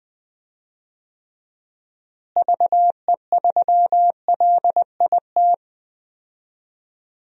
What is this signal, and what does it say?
Morse code, heard as a keyed tone, about 20 wpm, 700 Hz. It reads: VE3LIT
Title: DXCC-100 Call Signs - 6 Characters - Repeated 3x - 20wpm